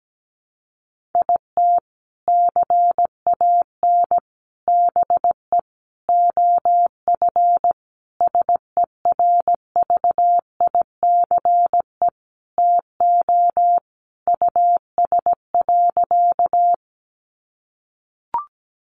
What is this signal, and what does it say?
Morse code, heard as a keyed tone, about 17 wpm, 700 Hz. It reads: IT CAN BE OF SERVICE TO US.
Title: Sentences from Top 1200 Words 17wpm